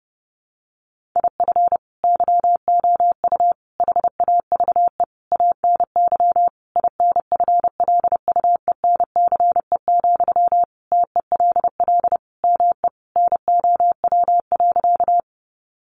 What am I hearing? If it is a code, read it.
Morse code, heard as a keyed tone, about 28 wpm, 700 Hz. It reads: IF YOU HAVE ANY INFLUENCE, TELL ME NOW.
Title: Sentences from Top 1100 Words 30wpm